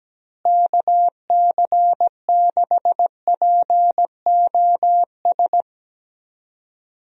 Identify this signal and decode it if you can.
Morse code, heard as a keyed tone, about 17 wpm, 700 Hz. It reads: KC6POS
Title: USA Call Signs - 6 Characters - Repeated 2x - 17wpm